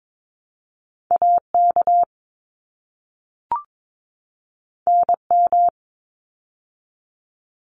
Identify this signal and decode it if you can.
Morse code, heard as a keyed tone, about 22 wpm, 700 Hz. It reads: AK NM
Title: ARRL Sections 22wpm